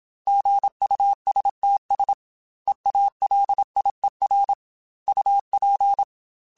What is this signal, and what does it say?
Morse code, heard as a keyed tone, about 27 wpm, 800 Hz. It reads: GUSTS EALIER UP
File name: Weather Sets / Weather Sets-mp3s / Weather 5 5 @25